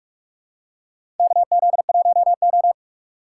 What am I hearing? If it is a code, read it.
Morse code, heard as a keyed tone, about 46 wpm, 700 Hz. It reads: KZ1O
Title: USA Call Signs - 4 Characters - 2x Character Spacing - 45wpm